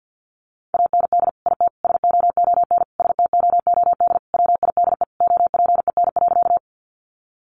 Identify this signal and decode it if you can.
Morse code, heard as a keyed tone, about 50 wpm, 700 Hz. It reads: AND IT SOON STOOD WIDE OPEN.
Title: Sentences from Top 900 Words 50wpm